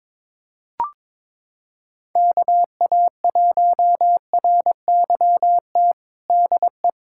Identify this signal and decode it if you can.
Morse code, heard as a keyed tone, about 21 wpm, 700 Hz. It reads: KA1RYT DE
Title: USA Callsign DE Callsign - 6 Characters - 22wpm